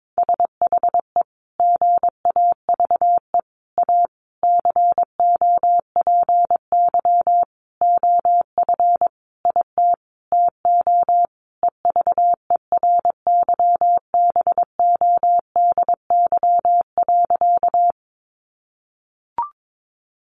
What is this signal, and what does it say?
Morse code, heard as a keyed tone, about 22 wpm, 700 Hz. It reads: SHE GAVE A COPY OF IT TO EVERYBODY.
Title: Sentences from Top 1400 Words 22wpm